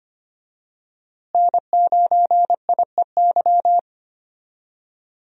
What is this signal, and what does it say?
Morse code, heard as a keyed tone, about 25 wpm, 700 Hz. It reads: N9IEY
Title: USA Call Signs - 5 Characters - Repeated 2x - 25wpm